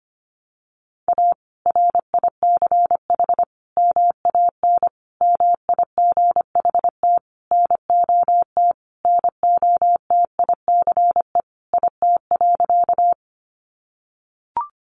Morse code, heard as a keyed tone, about 25 wpm, 700 Hz. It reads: A RICH MAN MIGHT NOT NOTICE IT.
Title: Sentences from Top 1100 Words 25wpm